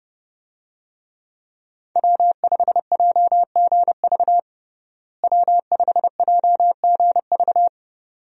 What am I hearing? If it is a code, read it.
Morse code, heard as a keyed tone, about 30 wpm, 700 Hz. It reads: W5JGV W5JGV
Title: USA Call Signs - 5 Characters - Repeated 2x - 30wpm